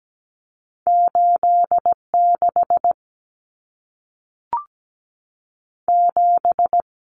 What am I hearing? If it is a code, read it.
Morse code, heard as a keyed tone, about 17 wpm, 700 Hz. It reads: 8 6 7
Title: Numbers - 2-Digit 17wpm